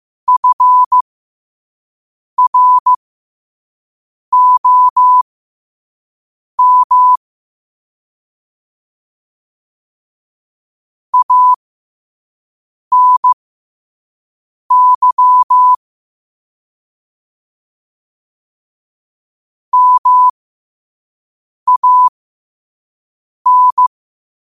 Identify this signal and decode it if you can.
Morse code, heard as a keyed tone, about 15 wpm, spaced as for 5 wpm, 1000 Hz. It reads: FROM ANY MAN
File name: qotdfiles / 5wpm